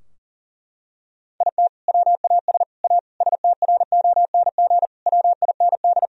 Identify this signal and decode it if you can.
Morse code, heard as a keyed tone, about 37 wpm, 700 Hz. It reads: IT WAS A STRONG WIND
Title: Sentences from Top 1000 Words - Review Part 2 - 40wpm